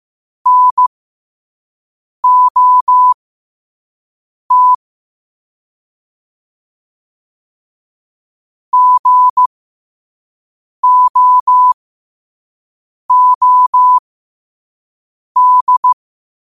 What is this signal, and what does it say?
Morse code, heard as a keyed tone, about 15 wpm, spaced as for 5 wpm, 1000 Hz. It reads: NOT GOOD
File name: qotdfiles / 5wpm